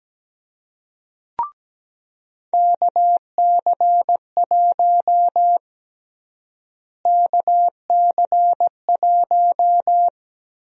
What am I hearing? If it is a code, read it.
Morse code, heard as a keyed tone, about 17 wpm, 700 Hz. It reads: KC1 KC1
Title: USA Call Sign Prefixes - Repeated 2x - 17wpm